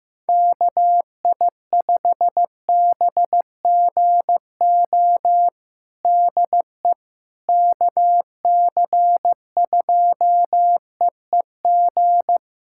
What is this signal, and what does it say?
Morse code, heard as a keyed tone, about 15 wpm, 700 Hz. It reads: KI5BGO DE KC2EEG
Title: USA Callsign DE Callsign 15wpm